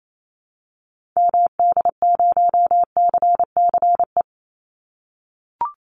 Morse code, heard as a keyed tone, about 28 wpm, 700 Hz. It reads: MD0CCE
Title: Call Signs - 28wpm